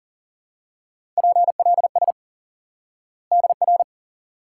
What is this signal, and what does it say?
Morse code, heard as a keyed tone, about 37 wpm, 700 Hz. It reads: PLS DR
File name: course / instant-qso-element-course-3 / IQE Course 3 - Lesson 89 - YAGI - Two Elements - 40wpm